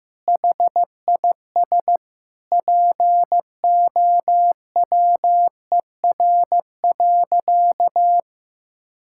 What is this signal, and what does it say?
Morse code, heard as a keyed tone, about 15 wpm, 700 Hz. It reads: HIS POWER.
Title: Sentences from Top 1600 Words 15wpm